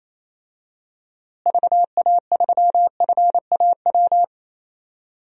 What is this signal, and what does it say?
Morse code, heard as a keyed tone, about 27 wpm, 700 Hz. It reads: VA3FAW
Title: DXCC-100 Call Signs - 6 Characters - Repeated 3x - 28wpm